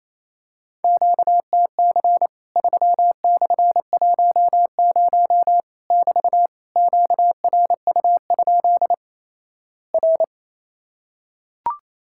Morse code, heard as a keyed tone, about 28 wpm, 700 Hz. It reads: QTC 3/10 <BT> QRU? R
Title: Worked All Europe DX Contest - QTC Exchange - 28wpm